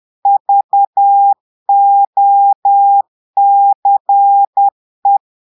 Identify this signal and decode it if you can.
Morse code, heard as a keyed tone, about 10 wpm, 800 Hz. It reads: VOCE
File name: portugais texte en clair 2 vitesse 10 mots minutes